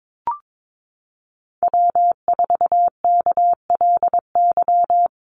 Morse code, heard as a keyed tone, about 22 wpm, 700 Hz. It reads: W4KLY
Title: Call Signs Repeated 2x - 22wpm